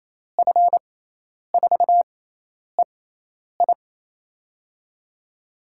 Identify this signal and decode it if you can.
Morse code, heard as a keyed tone, about 28 wpm, spaced as for 10 wpm, 700 Hz. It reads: F4EI
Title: USA Call Signs - 5 Characters - 6x Character Spacing - 28wpm